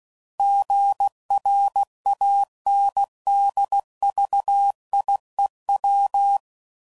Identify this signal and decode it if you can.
Morse code, heard as a keyed tone, about 16 wpm, 800 Hz. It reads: GRANDVIEW
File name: CW Code Groups @15WPM / fromSM's / Cities